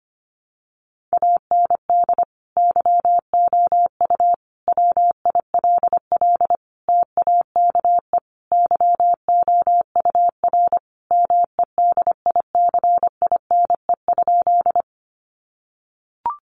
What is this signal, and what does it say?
Morse code, heard as a keyed tone, about 25 wpm, 700 Hz. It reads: AND YOU WILL TAKE YOUR MEDICINE?
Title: Sentences from Top 900 Words 25wpm